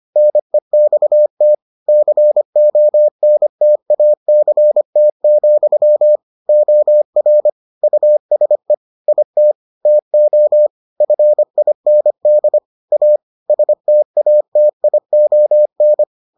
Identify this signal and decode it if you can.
Morse code, heard as a keyed tone, about 25 wpm, 600 Hz. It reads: NEXT CONTACT, OR USE IT TO FIND A STATION